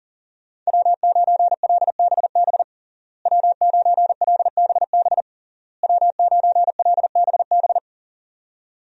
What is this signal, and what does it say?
Morse code, heard as a keyed tone, about 41 wpm, 700 Hz. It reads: W9LBB W9LBB W9LBB
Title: USA Call Signs - 5 Characters - Repeated 3x - 40wpm